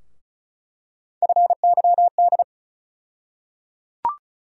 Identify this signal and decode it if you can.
Morse code, heard as a keyed tone, about 36 wpm, 700 Hz. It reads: FYD
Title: USA Call Sign Postfixes - 3 Letters 35wpm